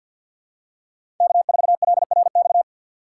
Morse code, heard as a keyed tone, about 48 wpm, 700 Hz. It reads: K4LRX
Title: K1USN SST QSO Practice - 50wpm